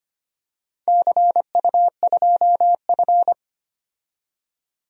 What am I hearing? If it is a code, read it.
Morse code, heard as a keyed tone, about 25 wpm, 700 Hz. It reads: CU2F
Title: DXCC-100 Call Signs - 4 Characters - 25wpm